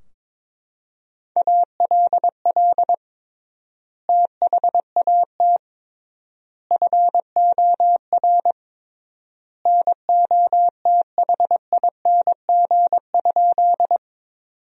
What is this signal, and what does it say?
Morse code, heard as a keyed tone, about 22 wpm, 700 Hz. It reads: ALL THAT FOR NOTHING?
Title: 3x Word-Spacing - Sentences from Top 500 Words - 22wpm